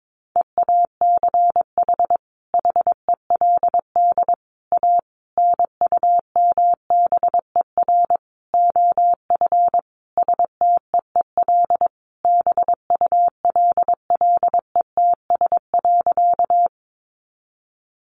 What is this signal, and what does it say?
Morse code, heard as a keyed tone, about 22 wpm, 700 Hz. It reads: EACH HELD A NUMBER OF STEEL BULLETS.
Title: A Sound Of Thunder 22wpm